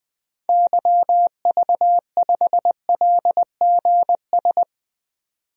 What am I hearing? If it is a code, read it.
Morse code, heard as a keyed tone, about 20 wpm, 700 Hz. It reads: YV5LGS
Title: DXCC-100 Call Signs - 6 Characters - 20wpm